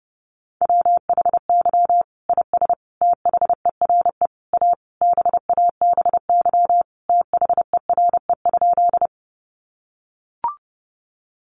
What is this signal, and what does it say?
Morse code, heard as a keyed tone, about 30 wpm, 700 Hz. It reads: WHY IS THERE A BABY THERE?
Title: Sentences from Top 1300 Words 30wpm